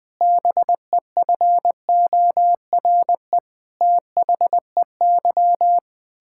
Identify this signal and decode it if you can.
Morse code, heard as a keyed tone, about 20 wpm, 700 Hz. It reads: BEFORE THEY
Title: Pangram Challenge v2 - 20wpm